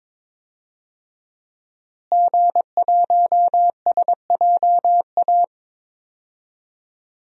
Morse code, heard as a keyed tone, about 22 wpm, 700 Hz. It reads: G1SJA
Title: DXCC-100 Call Signs - 5 Characters - Repeated 2x - 22wpm